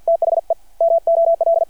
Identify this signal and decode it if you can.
Morse code, heard as a keyed tone, about 51 wpm, 660 Hz. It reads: THE MOL
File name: archived / CRSnet